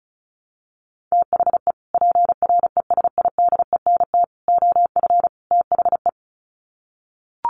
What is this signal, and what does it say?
Morse code, heard as a keyed tone, about 34 wpm, 700 Hz. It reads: THE PRESIDENT OF THE
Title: Sets of 4 Words 35wpm